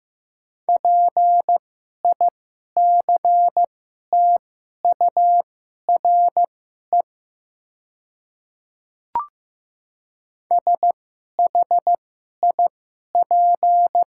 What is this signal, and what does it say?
Morse code, heard as a keyed tone, about 15 wpm, spaced as for 11 wpm, 700 Hz. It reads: PICTURE SHIP
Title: 2x Character Spacing - Top 500 Words - 15wpm